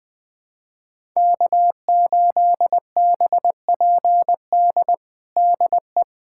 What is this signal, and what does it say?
Morse code, heard as a keyed tone, about 21 wpm, 700 Hz. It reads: K8BPD DE
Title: USA Callsign DE Callsign - 5 Characters - 20wpm